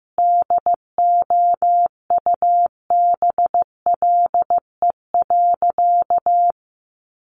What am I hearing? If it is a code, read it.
Morse code, heard as a keyed tone, about 15 wpm, 700 Hz. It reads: DOUBLE.
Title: Sentences from Top 1400 Words 15wpm